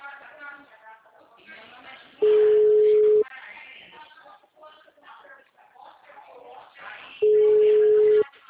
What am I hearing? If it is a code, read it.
Ringback tone (Europe, 425 Hz), with speech.